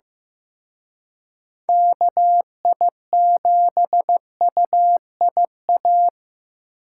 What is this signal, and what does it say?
Morse code, heard as a keyed tone, about 18 wpm, 700 Hz. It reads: KI7UIA
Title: Speed Racing - USA Call Signs 15wpm